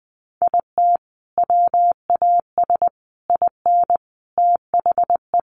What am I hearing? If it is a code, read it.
Morse code, heard as a keyed tone, about 20 wpm, 700 Hz. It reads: IT WAS IN THE
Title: Sentences from Top 1400 Words 20wpm